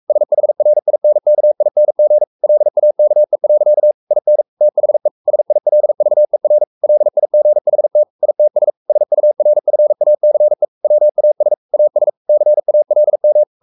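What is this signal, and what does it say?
Morse code, heard as a keyed tone, about 44 wpm, 600 Hz. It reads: SHRINKING LAKE. IN THE SILVER LIGHT ITS SURFACE WAS AS CALM
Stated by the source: Morse code at 43 wpm sending Ken Follett's novel Never in English, from CW med Gnister.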